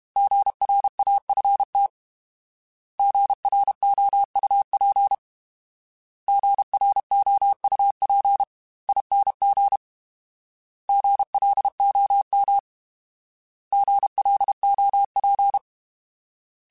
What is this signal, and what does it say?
Morse code, heard as a keyed tone, about 32 wpm, 800 Hz. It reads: GRAFT GROUP GROUPING GLOM GLOP